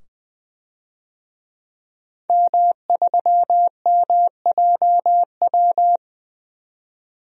Morse code, heard as a keyed tone, about 20 wpm, 700 Hz. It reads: M3MJW
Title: DXCC-100 Call Signs - 5 Characters - 20wpm